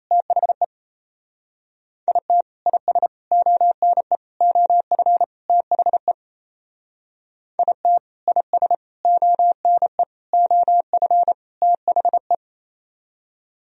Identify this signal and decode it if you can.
Morse code, heard as a keyed tone, about 32 wpm, 700 Hz. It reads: THE IT IS ONE OF THE IT IS ONE OF THE
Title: Speed-Racing - Sets of 5 Words 28wpm